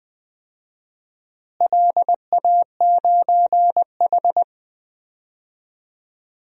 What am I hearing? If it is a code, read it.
Morse code, heard as a keyed tone, about 20 wpm, 700 Hz. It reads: LA9H
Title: DXCC-100 Call Signs - 4 Characters - Repeated 3x - 20wpm